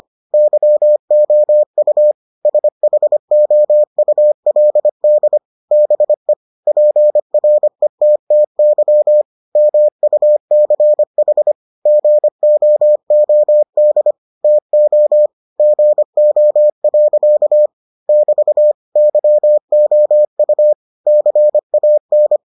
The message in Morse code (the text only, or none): YOU SHOULD BE PRETTY MUCH GOOD TO GO. <BT> YOU CAN